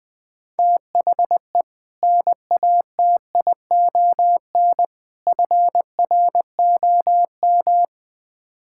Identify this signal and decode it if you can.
Morse code, heard as a keyed tone, about 20 wpm, 700 Hz. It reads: THE NATION FROM
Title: Build A Phrase - 5-Word Phrases - 20wpm